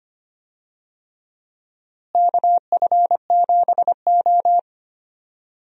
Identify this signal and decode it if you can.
Morse code, heard as a keyed tone, about 25 wpm, 700 Hz. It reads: KF7O
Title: USA Call Signs - 4 Characters - Repeated 3x - 25wpm